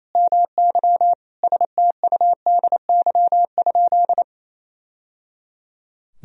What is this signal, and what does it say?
Morse code, heard as a keyed tone, about 28 wpm, 700 Hz. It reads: MY STUDY?
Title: Sentences from Top 1000 Words - Review Part 1 - 28wpm